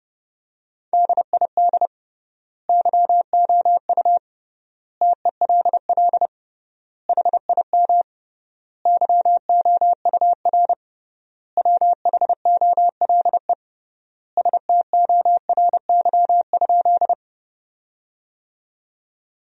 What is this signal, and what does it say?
Morse code, heard as a keyed tone, about 31 wpm, 700 Hz. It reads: DID YOU TELL HIM YOUR WHOLE STORY?
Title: 3x Word-Spacing - Sentences from Top 400 Words - 30wpm